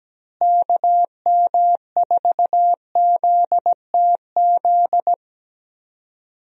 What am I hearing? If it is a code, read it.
Morse code, heard as a keyed tone, about 17 wpm, 700 Hz. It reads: KM4ZTZ
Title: USA Call Signs - 6 Characters - Repeated 3x - 17wpm